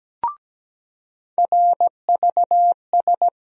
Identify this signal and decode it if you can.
Morse code, heard as a keyed tone, about 17 wpm, 700 Hz. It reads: RVS
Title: USA Call Sign Postfixes - 3 Letters 17wpm